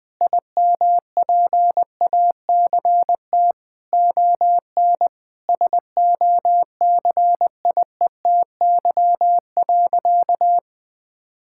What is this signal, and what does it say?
Morse code, heard as a keyed tone, about 21 wpm, 700 Hz. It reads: IMPACT ON SOCIETY.
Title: Sentences from Top 1000 Words - Review Part 6 - 20wpm